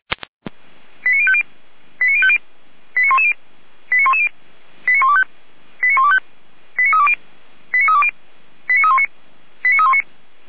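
Five-tone selective call (ZVEI), heard as quick runs of five tones, tones of about 70 ms. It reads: 89050 89050 89110 89110 89125 89125 89320 89320 89319 89319
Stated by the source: ZVEI five-tone call received on an SDR.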